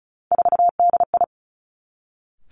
Morse code, heard as a keyed tone, about 36 wpm, 700 Hz. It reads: 4DI
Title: Three Challenging Letter-Number Combinations 35wpm